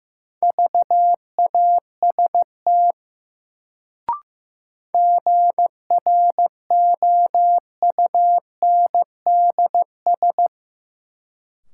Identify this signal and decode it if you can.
Morse code, heard as a keyed tone, about 15 wpm, 700 Hz. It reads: VAST GROUNDS
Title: Sentences from Top 1300 Words - New Word Drill - 15wpm